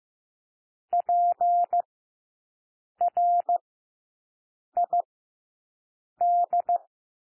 Morse code, heard as a keyed tone, about 15 wpm, 700 Hz. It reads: P R I D
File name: cwmp3 / W1 SOUND WORDS 15 wpm